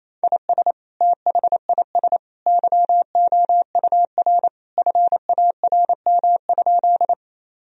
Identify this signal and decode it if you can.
Morse code, heard as a keyed tone, about 28 wpm, 700 Hz. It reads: IS THIS YOUR FARM?
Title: Sentences from Top 1000 Words - Review Part 2 - 28wpm